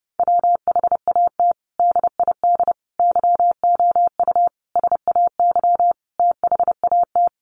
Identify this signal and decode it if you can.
Morse code, heard as a keyed tone, about 28 wpm, 700 Hz. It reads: WHAT DID YOU SAY THAT
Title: Sentences from Top 200 Words 30wpm